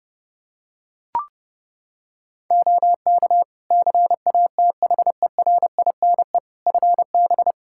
Morse code, heard as a keyed tone, about 30 wpm, 700 Hz. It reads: OK CATHERINE FB
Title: Practice QSO - Exchange 2 - Info - 30wpm